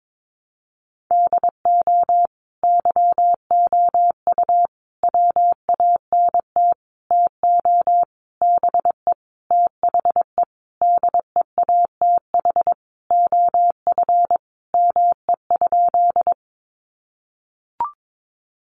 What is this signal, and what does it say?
Morse code, heard as a keyed tone, about 22 wpm, 700 Hz. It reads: DO YOU WANT TO BE THE DEATH OF ME?
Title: Sentences from Top 1100 Words 22wpm